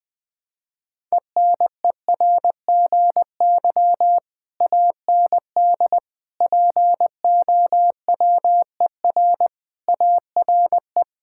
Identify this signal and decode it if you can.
Morse code, heard as a keyed tone, about 19 wpm, 700 Hz. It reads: ENERGY AND POWER ARE
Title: Sentences from Top 1000 Words - Review Part 6 - 20wpm